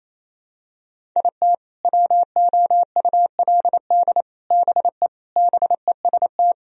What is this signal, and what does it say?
Morse code, heard as a keyed tone, about 28 wpm, 700 Hz. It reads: IT WOULD BE BEST
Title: Sentences from Top 1000 Words - Review Part 1 - 28wpm